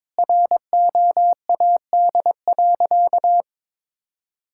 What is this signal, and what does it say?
Morse code, heard as a keyed tone, about 22 wpm, 700 Hz. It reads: ROAD.
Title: Sentences from Top 1600 Words 22wpm